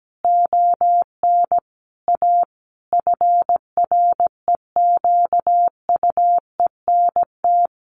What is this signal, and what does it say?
Morse code, heard as a keyed tone, about 17 wpm, 700 Hz. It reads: ON A FREQUENT
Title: Sentences from Top 1400 Words 17wpm